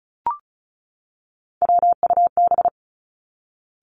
Morse code, heard as a keyed tone, about 36 wpm, 700 Hz. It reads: WUB
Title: Three Challenging Letter-Number Combinations 35wpm